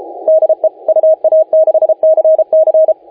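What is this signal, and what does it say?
Morse code, heard as a keyed tone, about 32 wpm, 630 Hz, with radio noise. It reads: DE UA6CC
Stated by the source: Morse contest audio recorded by LA4C during the SAC CW 2018 contest.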